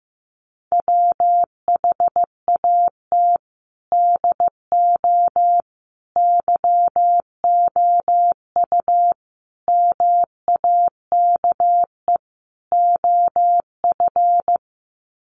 Morse code, heard as a keyed tone, about 15 wpm, 700 Hz. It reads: WHAT DO YOU MAKE OF
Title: Sets of 5 Words 15wpm